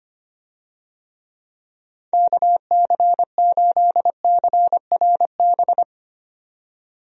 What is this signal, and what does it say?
Morse code, heard as a keyed tone, about 25 wpm, 700 Hz. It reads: KC8CRB
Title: USA Call Signs - 6 Characters - 25wpm